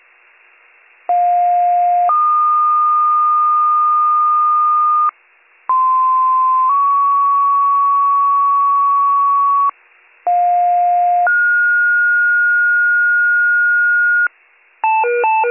Two-tone page, heard as a one-second tone, then a three-second tone, about 710 then 1220 Hz, about 1040 then 1120 Hz, about 710 then 1430 Hz. It is heard from 1.1 to 14.3 s.